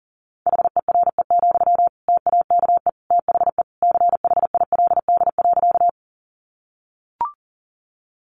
Morse code, heard as a keyed tone, about 37 wpm, 690 Hz. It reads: HERE, TAKE THE CHILD.
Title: Sentences from Top 800 Words 40wpm